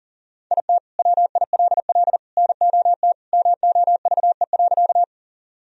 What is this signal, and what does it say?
Morse code, heard as a keyed tone, about 40 wpm, 700 Hz. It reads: IT WILL NOT MOVE.
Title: Sentences from Top 1000 Words - Review Part 1 - 40wpm